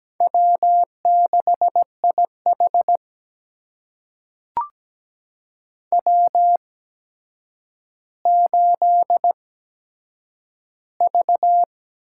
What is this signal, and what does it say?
Morse code, heard as a keyed tone, about 17 wpm, spaced as for 8 wpm, 700 Hz. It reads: W6IH W8V
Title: USA Call Signs - 8x Character Spacing - 17wpm